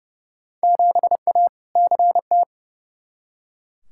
Morse code, heard as a keyed tone, about 30 wpm, 700 Hz. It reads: 7A CT
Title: Field Day Exchanges - 30wpm